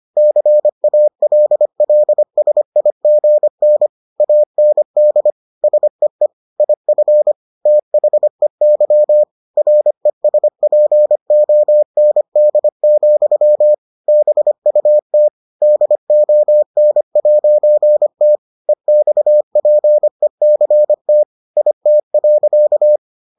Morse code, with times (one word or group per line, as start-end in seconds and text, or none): 0.2-3.9 CALLSIGN
4.2-5.3 AND
5.6-6.3 SEE
6.6-7.3 IF
7.7-9.2 THEY
9.6-13.7 RESPOND,
14.1-15.3 BUT
15.6-18.4 DON'T
18.7-21.2 EXPECT
21.6-23.0 IT.